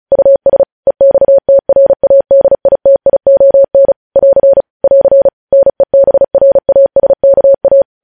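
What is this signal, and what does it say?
Morse code, heard as a keyed tone, about 36 wpm, 550 Hz. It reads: US EXTRADITION <AR> <AR> NEBRASKA
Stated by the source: News headlines sent in Morse at 35 WPM.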